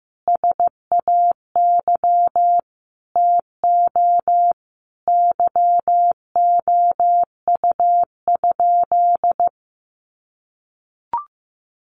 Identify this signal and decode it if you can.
Morse code, heard as a keyed tone, about 15 wpm, 700 Hz. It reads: SAY TO YOU?
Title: Sentences from Top 800 Words 15wpm